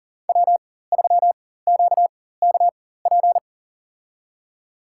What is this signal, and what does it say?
Morse code, heard as a keyed tone, about 40 wpm, spaced as for 19 wpm, 690 Hz. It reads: W3QKP
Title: USA Call Signs - 4x Character Spacing - 40wpm